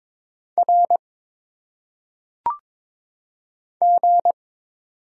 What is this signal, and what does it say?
Morse code, heard as a keyed tone, about 22 wpm, 700 Hz. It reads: R G
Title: Single Letters - Phonetic Pronunciation 22wpm